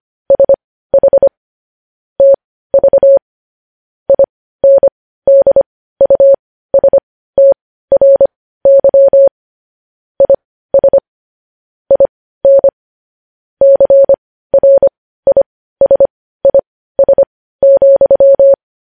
Morse code, heard as a keyed tone, about 25 wpm, spaced as for 15 wpm, 550 Hz. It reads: SH TV INDUSTRY IS IN CRISIS,
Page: News Headlines in Morse Code at 15 WPM